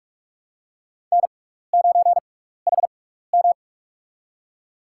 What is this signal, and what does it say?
Morse code, heard as a keyed tone, about 46 wpm, spaced as for 16 wpm, 700 Hz. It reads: N9HM
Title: USA Call Signs - 6x Character Spacing - 45wpm